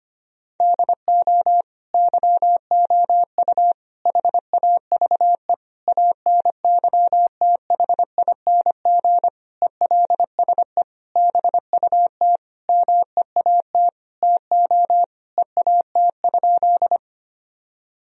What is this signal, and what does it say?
Morse code, heard as a keyed tone, about 25 wpm, 700 Hz. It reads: DO YOU HAVE ANYTHING ELSE BUT MEAT TO EAT?
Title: Sentences from Top 1000 Words - Review Part 3 - 25wpm